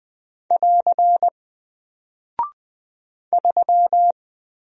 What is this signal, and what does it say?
Morse code, heard as a keyed tone, about 20 wpm, 700 Hz. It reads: <AR> 3
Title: Single Character (Letter:Number:Punctuation:Prosign) 20wpm